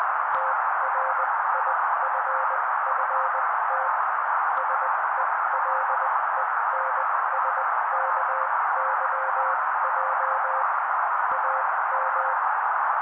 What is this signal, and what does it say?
Morse code, heard as a keyed tone, about 20 wpm, spaced as for 16 wpm, 530 Hz. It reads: TRIFFT SELENSKYJ AM
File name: daily / 16wpm noise